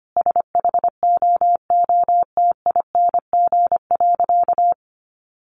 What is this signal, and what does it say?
Morse code, heard as a keyed tone, about 25 wpm, 700 Hz. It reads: SHOOTING.
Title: A Sound Of Thunder 25wpm